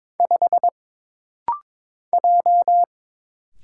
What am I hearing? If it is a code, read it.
Morse code, heard as a keyed tone, about 22 wpm, 700 Hz. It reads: 5 J
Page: This rapid-fire timing is geared to push your ICR (Instant Character Recognition) skills.